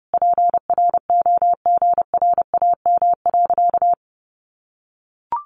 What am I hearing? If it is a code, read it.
Morse code, heard as a keyed tone, about 30 wpm, 700 Hz. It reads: PROGRAM.
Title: Sentences from Top 1500 Words 30wpm